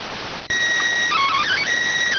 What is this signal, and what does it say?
SSTV (slow-scan television), PD 290.